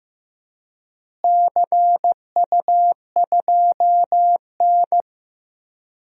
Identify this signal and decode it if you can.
Morse code, heard as a keyed tone, about 15 wpm, 700 Hz. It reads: CU2N
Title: DXCC-100 Call Signs - 4 Characters - Repeated 2x - 15wpm